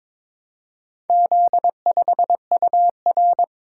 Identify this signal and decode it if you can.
Morse code, heard as a keyed tone, about 22 wpm, 700 Hz. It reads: Z5UR
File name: practice / 22wpm / Four Letter-Number Combinations - 22wpm